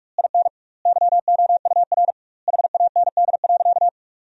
Morse code, heard as a keyed tone, about 46 wpm, 700 Hz. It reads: IN YOUR HAND.
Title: 2x Word-Spacing - Sentences from Top 400 Words - 45wpm